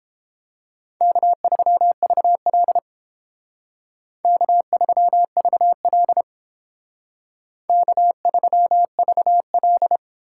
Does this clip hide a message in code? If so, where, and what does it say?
Yes: Morse code from 1.0 to 10.0 s, heard as a keyed tone, about 29 wpm, 700 Hz. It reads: K3VL K3VL K3VL